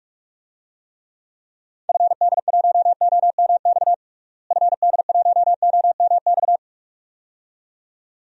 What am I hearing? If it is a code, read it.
Morse code, heard as a keyed tone, about 46 wpm, 700 Hz. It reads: FD1OMX FD1OMX
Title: DXCC-100 Call Signs - 6 Characters - Repeated 2x - 45wpm